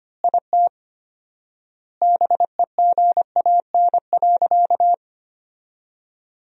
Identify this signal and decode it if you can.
Morse code, heard as a keyed tone, about 25 wpm, 700 Hz. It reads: IT BEGAN.
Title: 4x Word-Spacing - Sentences from Top 400 Words - 25wpm